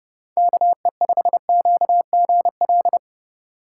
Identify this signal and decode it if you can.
Morse code, heard as a keyed tone, about 28 wpm, 700 Hz. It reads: KE5QGL